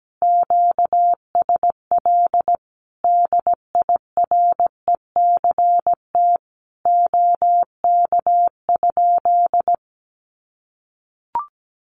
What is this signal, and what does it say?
Morse code, heard as a keyed tone, about 17 wpm, 700 Hz. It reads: QSL DIRECT OK?
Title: Common QSO Phrases 17wpm